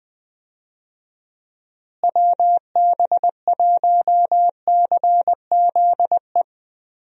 Morse code, heard as a keyed tone, about 21 wpm, 700 Hz. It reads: WB1CZE
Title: USA Call Signs - 6 Characters - Repeated 3x - 20wpm